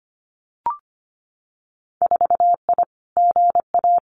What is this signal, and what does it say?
Morse code, heard as a keyed tone, about 25 wpm, 700 Hz. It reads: 4I GA
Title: Winter Field Day Exchanges Encore 25wpm